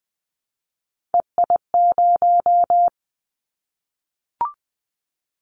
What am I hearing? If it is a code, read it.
Morse code, heard as a keyed tone, about 20 wpm, 700 Hz. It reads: EI0
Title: Callsign Prefixes for Top 50 DXCC - 20wpm